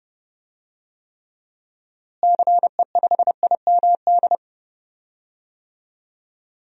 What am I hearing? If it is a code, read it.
Morse code, heard as a keyed tone, about 30 wpm, 700 Hz. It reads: CE5IMD
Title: DXCC-100 Call Signs - 6 Characters - Repeated 2x - 30wpm